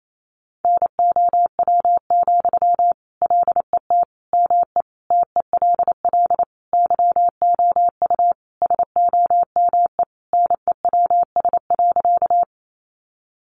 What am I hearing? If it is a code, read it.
Morse code, heard as a keyed tone, about 28 wpm, 700 Hz. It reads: NOW, LET ME TELL YOU SOME NEWS.
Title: Sentences from Top 800 Words 28wpm